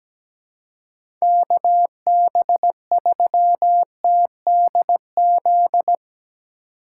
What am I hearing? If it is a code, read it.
Morse code, heard as a keyed tone, about 17 wpm, 700 Hz. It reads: KB3TDZ
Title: USA Call Signs - 6 Characters - 4x Character Spacing - 17wpm